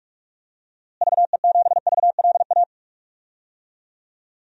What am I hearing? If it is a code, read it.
Morse code, heard as a keyed tone, about 46 wpm, 700 Hz. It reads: VE7VLA
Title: DXCC-100 Call Signs - 6 Characters - Repeated 2x - 45wpm